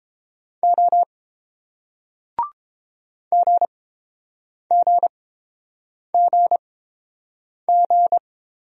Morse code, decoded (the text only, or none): O GGGG